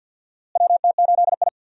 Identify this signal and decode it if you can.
Morse code, heard as a keyed tone, about 52 wpm, 700 Hz. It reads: WT8I
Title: USA Call Signs - 4 Characters - Repeated 2x - 50wpm